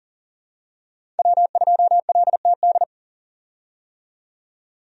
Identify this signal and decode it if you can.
Morse code, heard as a keyed tone, about 40 wpm, 700 Hz. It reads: W2LTD
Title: USA Call Signs - 5 Characters - 40wpm